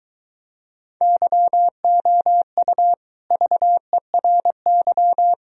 Morse code, heard as a keyed tone, about 23 wpm, 700 Hz. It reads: YOU VERY